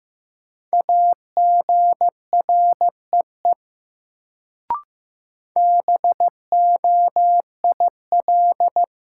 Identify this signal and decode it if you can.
Morse code, heard as a keyed tone, about 15 wpm, 700 Hz. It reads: AGREE BOIL
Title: Sets of 5 Words Encore - New Word Drill - 15wpm